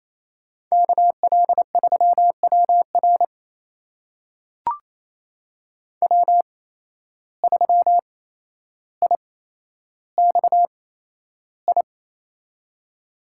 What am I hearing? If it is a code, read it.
Morse code, heard as a keyed tone, about 28 wpm, spaced as for 12 wpm, 700 Hz. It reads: KL3WR W3IXI
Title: USA Call Signs - 5 Characters - 8x Character Spacing - 28wpm